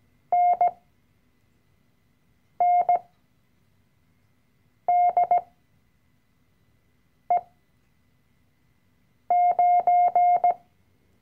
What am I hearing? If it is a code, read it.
Morse code, heard as a keyed tone, about 17 wpm, spaced as for 4 wpm, 710 Hz. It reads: NNDE9
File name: CW-Online-Kurs / Anfänger / Lektion 012